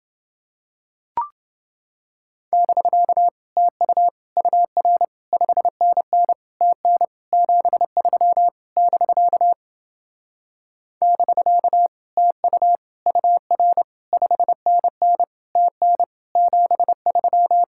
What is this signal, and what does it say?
Morse code, heard as a keyed tone, about 29 wpm, 700 Hz. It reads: <BK> TU UR 5NN TN 73 <BK> <BK> TU UR 5NN TN 73